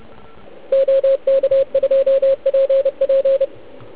Morse code, heard as a keyed tone, about 29 wpm, 540 Hz, with radio noise. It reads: OK2PP
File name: Archiv / CUC149 / OK1IF CUC149